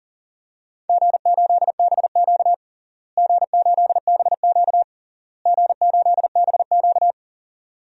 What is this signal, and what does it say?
Morse code, heard as a keyed tone, about 40 wpm, 700 Hz. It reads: G8BQ G8BQ G8BQ
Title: DXCC-100 Call Signs - 4 Characters - Repeated 3x - 40wpm